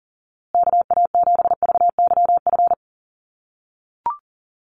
Morse code, heard as a keyed tone, about 40 wpm, 700 Hz. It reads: KA7VYF
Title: USA Call Signs Repeated 3x 40wpm